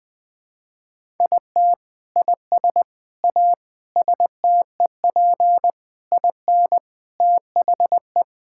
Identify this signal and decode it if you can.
Morse code, heard as a keyed tone, about 21 wpm, 700 Hz. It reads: IT IS A STEP IN THE
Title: Sentences with 8-Word Phrases - 20wpm